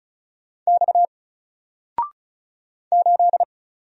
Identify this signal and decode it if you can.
Morse code, heard as a keyed tone, about 34 wpm, 700 Hz. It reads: X 8
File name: Single-Letter-Number-35wpm